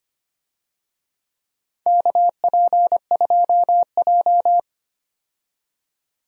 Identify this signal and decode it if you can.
Morse code, heard as a keyed tone, about 25 wpm, 700 Hz. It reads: KP2J
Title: DXCC-100 Call Signs - 4 Characters - 25wpm